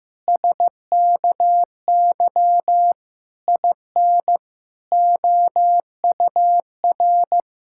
Morse code, heard as a keyed tone, about 15 wpm, 700 Hz. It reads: SKY IN OUR
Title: Pangram Challenge v2 - 15wpm